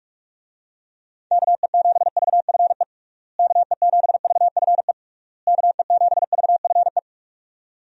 Morse code, heard as a keyed tone, about 46 wpm, 700 Hz. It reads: KE7VFE KE7VFE KE7VFE
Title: USA Call Signs - 6 Characters - Repeated 3x - 45wpm